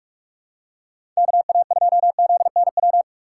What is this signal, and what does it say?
Morse code, heard as a keyed tone, about 46 wpm, 700 Hz. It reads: KA2ZNW
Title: USA Call Signs - 6 Characters - 2x Character Spacing - 45wpm